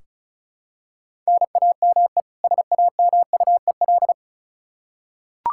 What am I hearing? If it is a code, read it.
Morse code, heard as a keyed tone, about 36 wpm, 700 Hz. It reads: NAME SAMUEL
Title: Practice QSO - Exchange 1 - Beginning - 35wpm